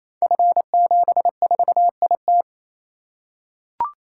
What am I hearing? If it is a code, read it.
Morse code, heard as a keyed tone, about 28 wpm, 700 Hz. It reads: F74IT
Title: Code Groups - 5 Letter:Numbers - 28wpm